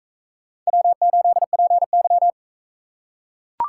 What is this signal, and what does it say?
Morse code, heard as a keyed tone, about 39 wpm, 700 Hz. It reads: W8PY